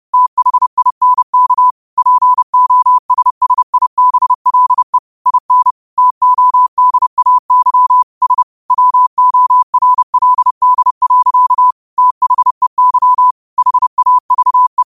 Morse code, heard as a keyed tone, about 30 wpm, 1000 Hz. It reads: THINK POSSIBLE IN TODAY S WORLD. THEY HAVE